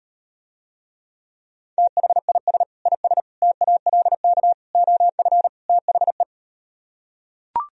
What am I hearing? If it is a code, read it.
Morse code, heard as a keyed tone, about 38 wpm, 700 Hz. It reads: THIS IS TALK OF THE